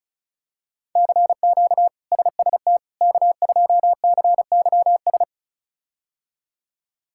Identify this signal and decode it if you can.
Morse code, heard as a keyed tone, about 34 wpm, 700 Hz. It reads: CQ SST K2CYS
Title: K1USN SST QSO Practice - 35wpm